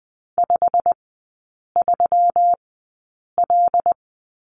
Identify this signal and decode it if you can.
Morse code, heard as a keyed tone, about 19 wpm, spaced as for 8 wpm, 700 Hz. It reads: 53L
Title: 138 Lesson 23e - Letter f - Three Characters 20wpm